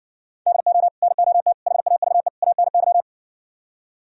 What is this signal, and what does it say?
Morse code, heard as a keyed tone, about 59 wpm, 700 Hz. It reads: DO NOT HAVE ANY